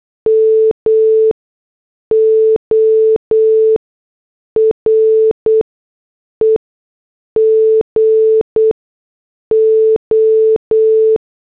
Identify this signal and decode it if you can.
Morse code, heard as a keyed tone, about 8 wpm, spaced as for 6 wpm, 440 Hz. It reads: MORE GO